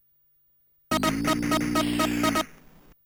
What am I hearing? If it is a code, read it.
There is no code.